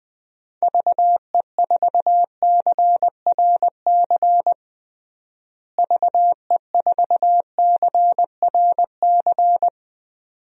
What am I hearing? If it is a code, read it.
Morse code, heard as a keyed tone, about 20 wpm, 700 Hz. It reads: VE4CRC VE4CRC
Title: DXCC-100 Call Signs - 6 Characters - Repeated 2x - 20wpm